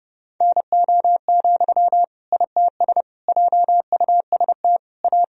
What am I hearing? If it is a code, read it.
Morse code, heard as a keyed tone, about 30 wpm, 700 Hz. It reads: NO, ITS JUST A
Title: Sentences from Top 1000 Words - Review Part 3 - 30wpm